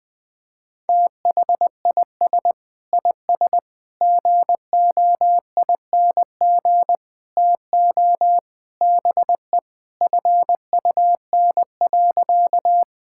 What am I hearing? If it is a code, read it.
Morse code, heard as a keyed tone, about 20 wpm, 700 Hz. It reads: THIS IS GOING TO BE FUN.
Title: Sentences with 4-Word Phrases - Spoken Beginning - 20wpm